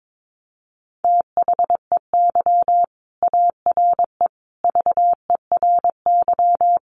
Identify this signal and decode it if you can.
Morse code, heard as a keyed tone, about 22 wpm, 700 Hz. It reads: THEY ARE VERY
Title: Sentences from Top 1500 Words 22wpm